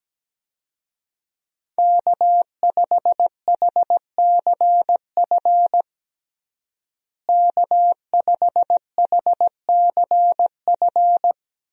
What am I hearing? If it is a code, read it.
Morse code, heard as a keyed tone, about 17 wpm, 700 Hz. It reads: K5HCF K5HCF
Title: USA Call Signs - 5 Characters - Repeated 3x - 17wpm